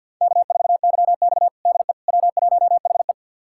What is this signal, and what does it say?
Morse code, heard as a keyed tone, about 52 wpm, 700 Hz. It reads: K4YX DE W1HE
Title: USA Callsign DE Callsign - 4 Characters - 50wpm